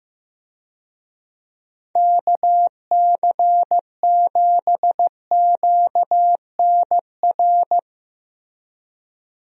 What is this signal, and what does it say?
Morse code, heard as a keyed tone, about 15 wpm, 700 Hz. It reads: KC7QNR
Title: USA Call Signs - 6 Characters - 15wpm